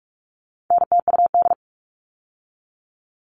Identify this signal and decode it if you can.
Morse code, heard as a keyed tone, about 44 wpm, 700 Hz. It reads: NTUD
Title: All Cut Numbers - 4-Digit 45wpm